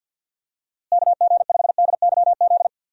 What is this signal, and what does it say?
Morse code, heard as a keyed tone, about 50 wpm, 700 Hz. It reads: KG5DYZ
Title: USA Call Signs - 6 Characters - Repeated 2x - 50wpm